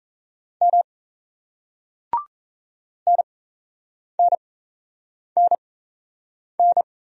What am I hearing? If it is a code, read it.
Morse code, heard as a keyed tone, about 37 wpm, spaced as for 8 wpm, 700 Hz. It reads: M NNNN